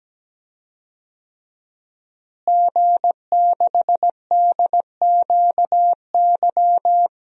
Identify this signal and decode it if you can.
Morse code, heard as a keyed tone, about 17 wpm, 700 Hz. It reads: G6DQY
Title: DXCC-100 Call Signs - 5 Characters - Repeated 2x - 17wpm